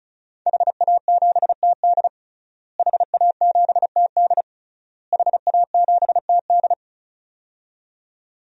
Morse code, heard as a keyed tone, about 34 wpm, 700 Hz. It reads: HA7TD HA7TD HA7TD
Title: DXCC-100 Call Signs - 5 Characters - Repeated 3x - 35wpm